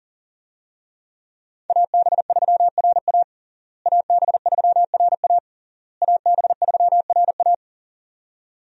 Morse code, heard as a keyed tone, about 37 wpm, 700 Hz. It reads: AB3RA AB3RA AB3RA
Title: USA Call Signs - 5 Characters - Repeated 3x - 40wpm